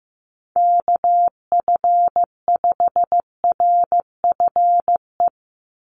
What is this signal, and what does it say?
Morse code, heard as a keyed tone, about 15 wpm, 700 Hz. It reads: KF5RFE
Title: USA Call Signs 15wpm